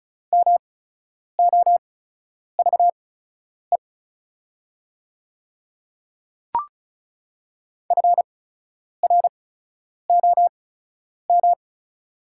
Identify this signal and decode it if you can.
Morse code, heard as a keyed tone, about 36 wpm, spaced as for 10 wpm, 700 Hz. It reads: MOVE FROM
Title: 8x Character Spacing - Top 500 Words - 35wpm